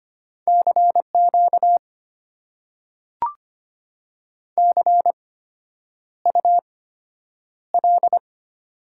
Morse code, heard as a keyed tone, about 25 wpm, spaced as for 9 wpm, 700 Hz. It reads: CQ CUL
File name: Top 90 QSO Elements - 8x Character Spacing - 25wpm